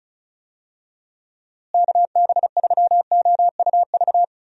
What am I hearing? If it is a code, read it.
Morse code, heard as a keyed tone, about 34 wpm, 700 Hz. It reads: KB3OUV
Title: USA Call Signs - 6 Characters - Repeated 2x - 35wpm